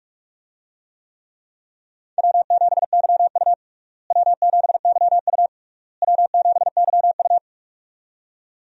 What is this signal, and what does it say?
Morse code, heard as a keyed tone, about 46 wpm, 700 Hz. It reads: W7YU W7YU W7YU
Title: USA Call Signs - 4 Characters - Repeated 3x - 45wpm